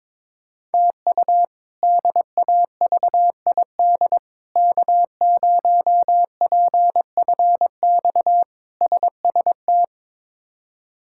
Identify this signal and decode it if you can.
Morse code, heard as a keyed tone, about 22 wpm, 700 Hz. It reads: TU DAVID K0PFX SST
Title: K1USN SST QSO Practice - 22wpm